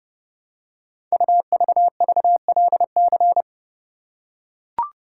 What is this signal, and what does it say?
Morse code, heard as a keyed tone, about 28 wpm, 700 Hz. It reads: UVVLC
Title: Code Groups - 5 Letters - 30wpm